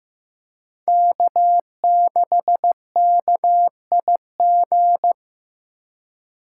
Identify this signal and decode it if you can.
Morse code, heard as a keyed tone, about 18 wpm, 700 Hz. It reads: K6KIG
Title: Speed Racing - USA 1x3 Call Signs 15wpm